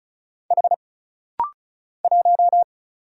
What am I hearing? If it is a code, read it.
Morse code, heard as a keyed tone, about 34 wpm, 700 Hz. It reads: H 1
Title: Single Letter-Number Alt - Rapid-Fire 35wpm